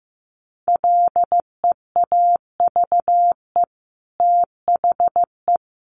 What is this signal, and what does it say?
Morse code, heard as a keyed tone, about 15 wpm, 700 Hz. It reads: LEAVE THE
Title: Sentences from Top 1200 Words 15wpm